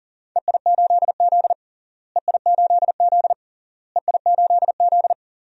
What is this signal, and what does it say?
Morse code, heard as a keyed tone, about 37 wpm, 700 Hz. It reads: EI8Z EI8Z EI8Z
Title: DXCC-100 Call Signs - 4 Characters - Repeated 3x - 40wpm